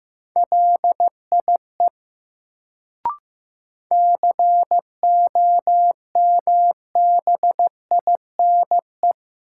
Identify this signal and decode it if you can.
Morse code, heard as a keyed tone, about 15 wpm, 700 Hz. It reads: LIE COMBINE
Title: Sentences from Top 1600 Words - New Word Drill Alt - 15wpm